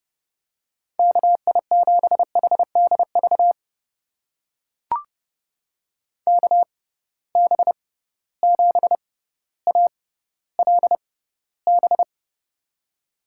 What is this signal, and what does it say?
Morse code, heard as a keyed tone, about 32 wpm, spaced as for 16 wpm, 700 Hz. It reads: KI7HDV KB7ALB
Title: USA Call Signs - 6x Character Spacing - 30wpm